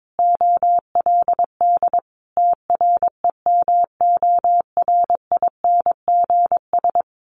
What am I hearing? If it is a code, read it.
Morse code, heard as a keyed tone, about 22 wpm, 700 Hz. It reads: OLD TREMORINGS
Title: A Sound Of Thunder 22wpm